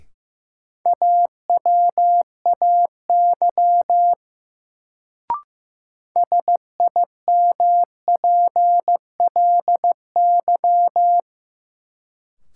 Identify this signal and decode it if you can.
Morse code, heard as a keyed tone, about 15 wpm, 700 Hz. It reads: AWAY SIMPLY
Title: Sentences from Top 600 Words - New Word Drill - 15wpm